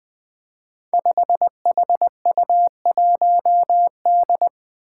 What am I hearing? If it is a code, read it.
Morse code, heard as a keyed tone, about 20 wpm, 700 Hz. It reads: 5HU1D
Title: Five Challenging Letter-Number Combinations 20wpm